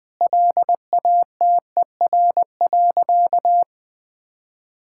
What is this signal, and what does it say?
Morse code, heard as a keyed tone, about 19 wpm, 700 Hz. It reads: LATER.
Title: Sentences from Top 1000 Words - Review Part 5 - 20wpm